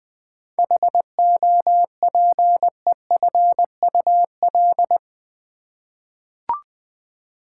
Morse code, heard as a keyed tone, about 20 wpm, 700 Hz. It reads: HOPEFUL
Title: 7 Letter Words - First 3 Letters Given - 20wpm